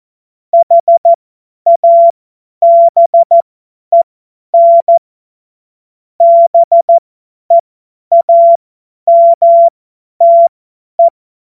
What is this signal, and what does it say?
Morse code, heard as a keyed tone, about 14 wpm, spaced as for 10 wpm, 680 Hz. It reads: HABEN BEAMTE
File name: Witz-21-Oktober-2025-50-F70